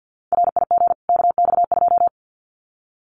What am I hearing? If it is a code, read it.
Morse code, heard as a keyed tone, about 50 wpm, 700 Hz. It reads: RIG KX2
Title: Common QSO Phrases 50wpm